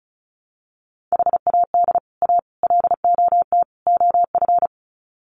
Morse code, heard as a keyed tone, about 34 wpm, 700 Hz. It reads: HAD A LOT OF
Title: Sets of 4 Words 35wpm